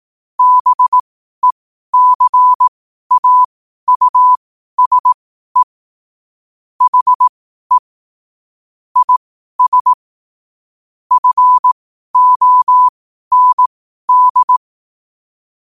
Morse code, heard as a keyed tone, about 18 wpm, spaced as for 12 wpm, 1000 Hz. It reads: BECAUSE HE IS FOND
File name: qotdfiles / 12wpm